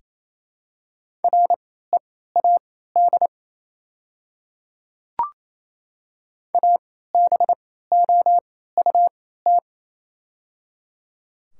Morse code, heard as a keyed tone, about 28 wpm, spaced as for 16 wpm, 700 Hz. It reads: READ ABOUT
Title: 3x Character Spacing - Top 200 Words - 28wpm